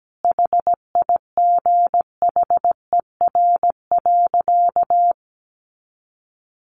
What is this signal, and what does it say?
Morse code, heard as a keyed tone, about 17 wpm, 700 Hz. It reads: HIGHER.
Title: Additional Easy Phrases 17wpm